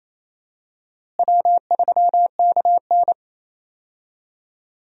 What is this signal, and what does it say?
Morse code, heard as a keyed tone, about 28 wpm, 700 Hz. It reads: W3KN
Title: USA Call Signs - 4 Characters - Repeated 2x - 28wpm